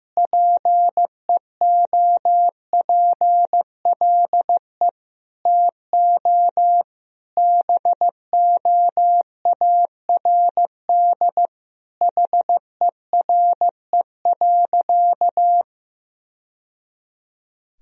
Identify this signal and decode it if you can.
Morse code, heard as a keyed tone, about 15 wpm, 700 Hz. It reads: PEOPLE TO BOARD HERE.
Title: Sentences from Top 1000 Words - Review Part 4 - 15wpm